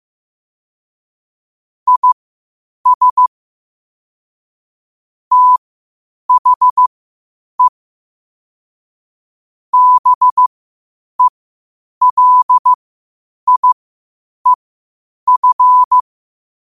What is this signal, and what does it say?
Morse code, heard as a keyed tone, about 15 wpm, spaced as for 8 wpm, 1000 Hz. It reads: IS THE BELIEF
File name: qotdfiles / 8wpm